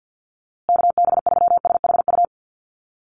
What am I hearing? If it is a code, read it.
Morse code, heard as a keyed tone, about 50 wpm, 700 Hz. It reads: KB3SHU
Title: USA Call Signs Repeated 3x 50wpm